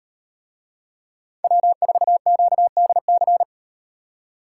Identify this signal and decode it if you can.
Morse code, heard as a keyed tone, about 36 wpm, 700 Hz. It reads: W4QDC